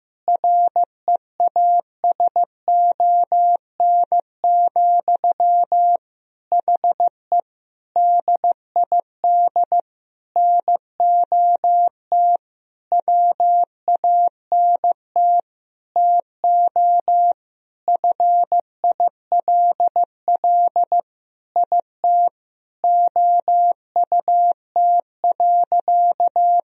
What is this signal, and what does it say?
Morse code, heard as a keyed tone, about 16 wpm, 700 Hz. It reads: REASON, HE DID NOT WANT TO FILL IT OUT.
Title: Sentences from Top 1000 Words - Review Part 3 - 15wpm